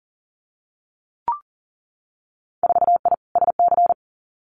Morse code, heard as a keyed tone, about 40 wpm, 700 Hz. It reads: 4I SC
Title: Winter Field Day Exchanges 40wpm